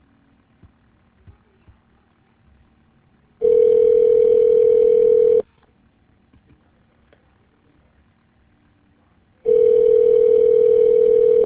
Ringback tone (North America), 440 + 480 Hz.